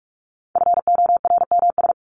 Morse code, heard as a keyed tone, about 44 wpm, 700 Hz. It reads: FORMS
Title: ES:S Suffix - Top 100 Words with Base Words 45wpm